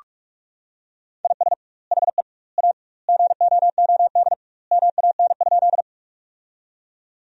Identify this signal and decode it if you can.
Morse code, heard as a keyed tone, about 44 wpm, 700 Hz. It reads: IS HE A GOOD MAN?
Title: 2x Word-Spacing - Sentences from Top 200 Words - 45wpm